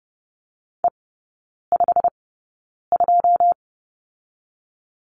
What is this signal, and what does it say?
Morse code, heard as a keyed tone, about 30 wpm, spaced as for 9 wpm, 700 Hz. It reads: E52
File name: course / 30:10wpm / 087 Lesson 15e - Number 5 - Three Characters 30wpm